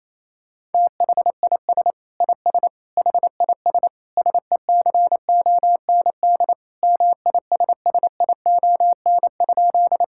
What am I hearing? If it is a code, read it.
Morse code, heard as a keyed tone, about 28 wpm, 700 Hz. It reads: THIS IS HIS SECOND MISSION?
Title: Sentences from Top 1600 Words 28wpm